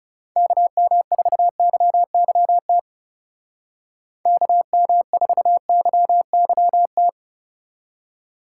Morse code, heard as a keyed tone, about 35 wpm, 700 Hz. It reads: KM4YYT KM4YYT
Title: Speed Racing - USA 2x3 Call Signs 30wpm